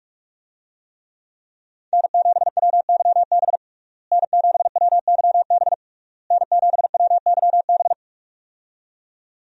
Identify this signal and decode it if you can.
Morse code, heard as a keyed tone, about 46 wpm, 700 Hz. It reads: N7WYB N7WYB N7WYB
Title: USA Call Signs - 5 Characters - Repeated 3x - 45wpm